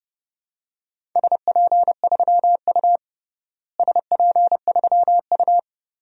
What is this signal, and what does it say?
Morse code, heard as a keyed tone, about 30 wpm, 700 Hz. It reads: SP3U SP3U
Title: DXCC-100 Call Signs - 4 Characters - Repeated 3x - 30wpm